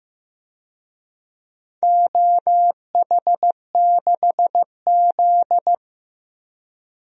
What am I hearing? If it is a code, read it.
Morse code, heard as a keyed tone, about 15 wpm, 700 Hz. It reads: OH6Z
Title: DXCC-100 Call Signs - 4 Characters - 15wpm